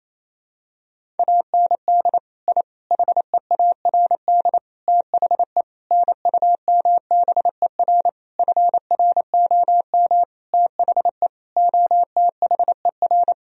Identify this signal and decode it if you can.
Morse code, heard as a keyed tone, about 28 wpm, 700 Hz. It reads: AND I HEARD THE NUMBER FROM THE OTHER
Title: Sentences from Top 1000 Words - Review Part 1 - 28wpm